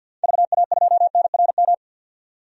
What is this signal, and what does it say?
Morse code, heard as a keyed tone, about 51 wpm, 700 Hz. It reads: VA2NRM
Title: DXCC-100 Call Signs - 6 Characters - 50wpm